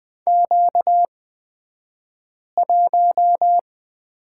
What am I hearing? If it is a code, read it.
Morse code, heard as a keyed tone, about 20 wpm, spaced as for 5 wpm, 700 Hz. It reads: Q1
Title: 235 Lesson 38d - Letter q - Two Characters 20wpm